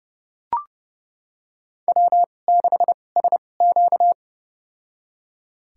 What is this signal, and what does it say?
Morse code, heard as a keyed tone, about 30 wpm, spaced as for 22 wpm, 700 Hz. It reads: W6SQ
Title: USA Call Signs - 4 Characters - 2x Character Spacing - 30wpm